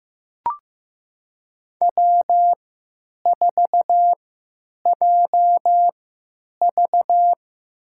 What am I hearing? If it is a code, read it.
Morse code, heard as a keyed tone, about 15 wpm, spaced as for 9 wpm, 700 Hz. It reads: W4JV
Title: USA Call Signs - 4 Characters - 3x Character Spacing - 15wpm